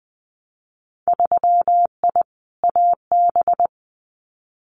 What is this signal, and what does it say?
Morse code, heard as a keyed tone, about 20 wpm, 700 Hz. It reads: 3I AB
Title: Winter Field Day Exchanges Encore 20wpm